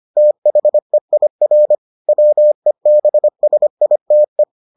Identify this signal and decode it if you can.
Morse code, heard as a keyed tone, about 25 wpm, 600 Hz. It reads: THEIR WEBSITE